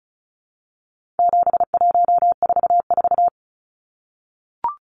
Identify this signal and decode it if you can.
Morse code, heard as a keyed tone, about 34 wpm, 700 Hz. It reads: 7144
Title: Numbers - 4-Digit 35wpm